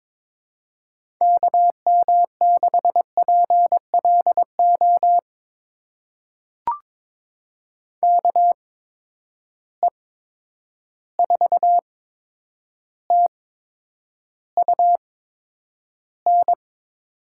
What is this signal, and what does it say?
Morse code, heard as a keyed tone, about 21 wpm, spaced as for 9 wpm, 700 Hz. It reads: KM6PLO KE4TUN
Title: USA Call Signs - 6 Characters - 8x Character Spacing - 22wpm